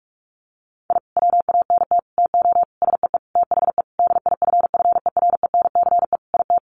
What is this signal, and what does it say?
Morse code, heard as a keyed tone, about 44 wpm, 700 Hz. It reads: I WANT TO SEE THE DIFFERENCE IT
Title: Sentences from Top 1300 Words 45wpm